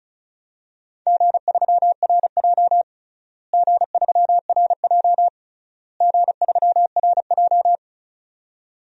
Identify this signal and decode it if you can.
Morse code, heard as a keyed tone, about 34 wpm, 700 Hz. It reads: G3RJ G3RJ G3RJ
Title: DXCC-100 Call Signs - 4 Characters - Repeated 3x - 35wpm